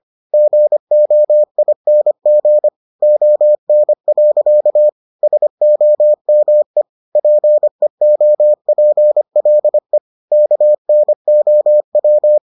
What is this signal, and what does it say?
Morse code, heard as a keyed tone, about 25 wpm, 600 Hz. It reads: GOING ON. SOME PEOPLE KNOW